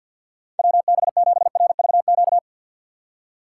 Morse code, heard as a keyed tone, about 52 wpm, 700 Hz. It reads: WB7RVQ
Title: USA Call Signs - 6 Characters - 4x Character Spacing - 50wpm